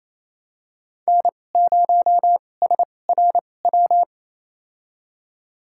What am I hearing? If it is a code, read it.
Morse code, heard as a keyed tone, about 28 wpm, spaced as for 20 wpm, 700 Hz. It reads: N0SRW
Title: USA Call Signs - 5 Characters - 2x Character Spacing - 28wpm